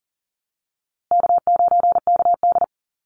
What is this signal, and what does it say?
Morse code, heard as a keyed tone, about 40 wpm, 700 Hz. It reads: K9KD
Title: USA Call Signs Repeated 3x 40wpm